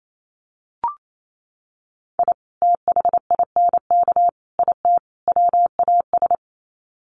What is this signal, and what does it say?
Morse code, heard as a keyed tone, about 28 wpm, 700 Hz. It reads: I THINK IT WAS
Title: Sets of 4 Words 28wpm